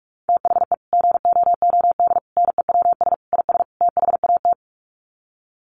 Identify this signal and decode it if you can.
Morse code, heard as a keyed tone, about 44 wpm, 700 Hz. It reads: THE GOOD NEWS IS THAT
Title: Sets of 5 Words 45wpm